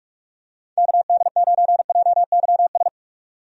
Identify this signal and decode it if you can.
Morse code, heard as a keyed tone, about 46 wpm, 700 Hz. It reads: KD9JYS
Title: USA Call Signs - 6 Characters - 45wpm